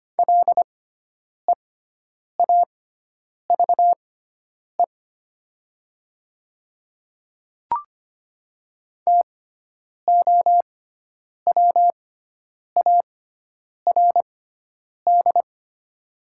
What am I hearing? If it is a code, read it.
Morse code, heard as a keyed tone, about 25 wpm, spaced as for 9 wpm, 700 Hz. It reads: LEAVE TOWARD
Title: 6x Character Spacing - Top 500 Words - 25wpm